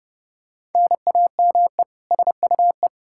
Morse code, heard as a keyed tone, about 30 wpm, 700 Hz. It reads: NAME SUE
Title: Practice QSO - Exchange 1 - Beginning - 30wpm